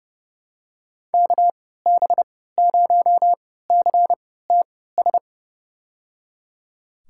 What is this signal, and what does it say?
Morse code, heard as a keyed tone, about 29 wpm, spaced as for 17 wpm, 700 Hz. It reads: KB0CTS
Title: USA Call Signs - 6 Characters - 3x Character Spacing - 30wpm